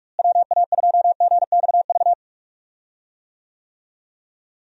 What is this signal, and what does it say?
Morse code, heard as a keyed tone, about 42 wpm, 700 Hz. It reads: WA2GXV
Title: USA Call Signs - 6 Characters - Repeated 2x - 45wpm